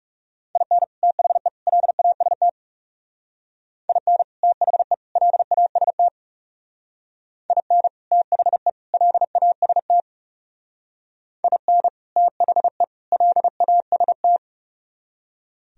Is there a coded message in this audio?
Yes: Morse code from 0.6 to 14.4 s, heard as a keyed tone, about 38 wpm, 700 Hz. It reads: IN THE LAST IN THE LAST IN THE LAST IN THE LAST